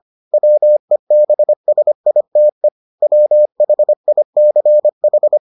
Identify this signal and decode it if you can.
Morse code, heard as a keyed tone, about 25 wpm, 600 Hz. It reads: WEBSITE WHICH